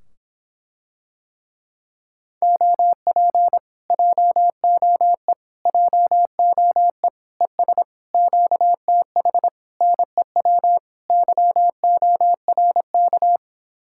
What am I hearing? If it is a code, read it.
Morse code, heard as a keyed tone, about 26 wpm, 700 Hz. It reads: OP JOE JOE ES QTH NEW YORK